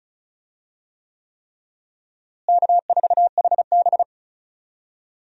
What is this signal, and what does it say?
Morse code, heard as a keyed tone, about 36 wpm, 700 Hz. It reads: K4HB
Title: USA Call Signs - 4 Characters - Repeated 3x - 35wpm